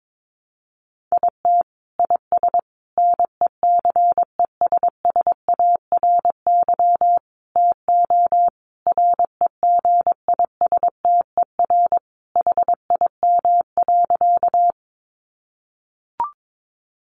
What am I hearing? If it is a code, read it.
Morse code, heard as a keyed tone, about 22 wpm, 700 Hz. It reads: IT IS NECESSARY TO REGISTER HIM.
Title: Sentences from Top 1200 Words 22wpm